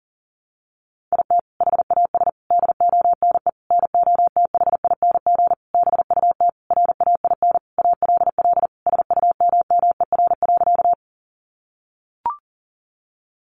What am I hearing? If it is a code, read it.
Morse code, heard as a keyed tone, about 37 wpm, 700 Hz. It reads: IT HAS DONE NOTHING BUT RAIN ALL SUMMER.
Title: Sentences from Top 1100 Words 40wpm